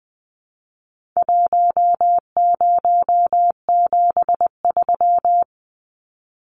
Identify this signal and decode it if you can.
Morse code, heard as a keyed tone, about 20 wpm, 700 Hz. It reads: 1073
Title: Numbers - 4-Digit 20wpm